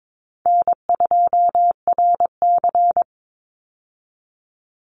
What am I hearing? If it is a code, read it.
Morse code, heard as a keyed tone, about 22 wpm, 700 Hz. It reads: N2RC
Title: Easy Call Signs Repeated 3x - 22wpm